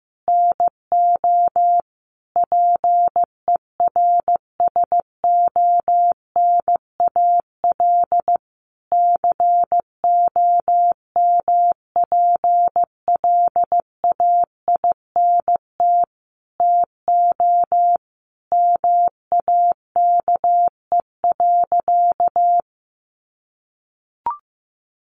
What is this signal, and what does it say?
Morse code, heard as a keyed tone, about 15 wpm, 700 Hz. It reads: NO PERSONAL COMPLAINT TO MAKE.
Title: Sentences from Top 1400 Words 15wpm